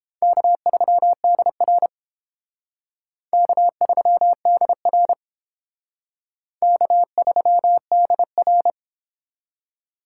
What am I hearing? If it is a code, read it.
Morse code, heard as a keyed tone, about 28 wpm, 700 Hz. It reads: K3DR K3DR K3DR